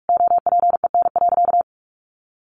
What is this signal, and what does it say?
Morse code, heard as a keyed tone, about 44 wpm, 700 Hz. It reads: OPEN.
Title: Sentences from Top 1000 Words 45wpm